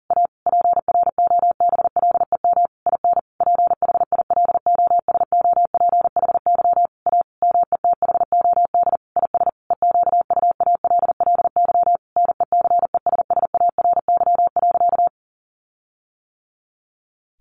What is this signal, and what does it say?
Morse code, heard as a keyed tone, about 40 wpm, 700 Hz. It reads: A PROBLEM IN PHILOSOPHY A METHOD IS EQUALLY NECESSARY.
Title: Sentences from Top 1400 Words 40wpm